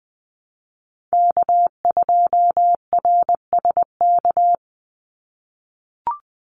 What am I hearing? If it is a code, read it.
Morse code, heard as a keyed tone, about 20 wpm, 700 Hz. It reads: K2RSK
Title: Call Signs - 20wpm